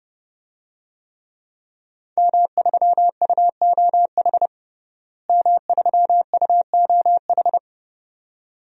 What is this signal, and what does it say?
Morse code, heard as a keyed tone, about 30 wpm, 700 Hz. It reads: M3UOH M3UOH
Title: DXCC-100 Call Signs - 5 Characters - Repeated 2x - 30wpm